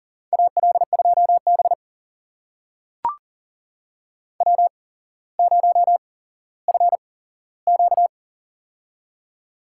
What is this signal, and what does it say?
Morse code, heard as a keyed tone, about 38 wpm, spaced as for 17 wpm, 700 Hz. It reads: AL2B W0FQ
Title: USA Call Signs - 8x Character Spacing - 40wpm